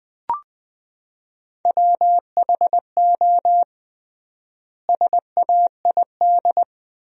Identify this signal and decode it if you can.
Morse code, heard as a keyed tone, about 21 wpm, 700 Hz. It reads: WHO SAID
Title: 3x Word-Spacing - Sentences from Top 300 Words - 20wpm